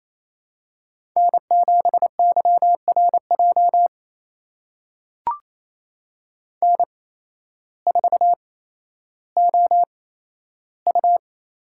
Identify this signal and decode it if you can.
Morse code, heard as a keyed tone, about 27 wpm, spaced as for 13 wpm, 700 Hz. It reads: N7YRJ N4OU
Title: USA Call Signs - 5 Characters - 8x Character Spacing - 28wpm